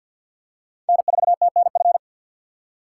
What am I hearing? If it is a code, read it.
Morse code, heard as a keyed tone, about 53 wpm, 700 Hz. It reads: N4TNF
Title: USA Call Signs - 4x Character Spacing - 50wpm